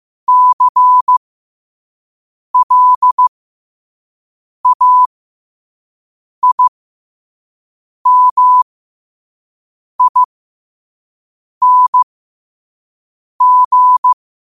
Morse code, heard as a keyed tone, about 15 wpm, spaced as for 5 wpm, 1000 Hz. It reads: CLAIMING
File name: qotdfiles / 5wpm